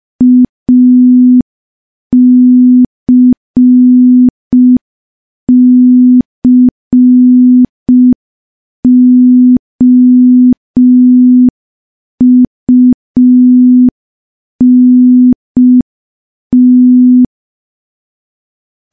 Morse code, heard as a keyed tone, about 5 wpm, 250 Hz. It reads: ACCOUNT
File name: morse / morse-training-05WPM